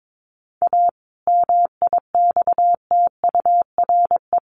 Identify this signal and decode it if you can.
Morse code, heard as a keyed tone, about 22 wpm, 700 Hz. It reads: A MIXTURE
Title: Sentences from Top 1500 Words 22wpm